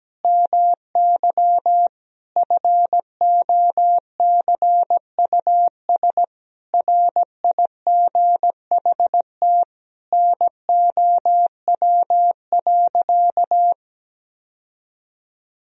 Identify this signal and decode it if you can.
Morse code, heard as a keyed tone, about 17 wpm, 700 Hz. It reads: MY FOCUS RIGHT NOW.
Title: Sentences from Top 1000 Words - Review Part 4 - 17wpm